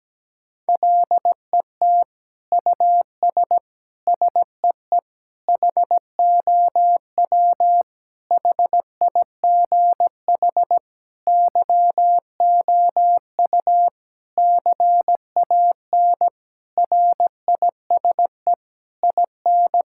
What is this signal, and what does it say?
Morse code, heard as a keyed tone, about 17 wpm, 700 Hz. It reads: LET US SEE HOW HIGH YOU CAN RISE IN
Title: Sentences from Top 1000 Words - Review Part 6 - 17wpm